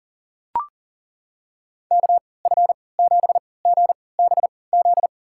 Morse code, heard as a keyed tone, about 37 wpm, spaced as for 22 wpm, 700 Hz. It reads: KF7GBZ
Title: USA Call Signs - 6 Characters - 3x Character Spacing - 40wpm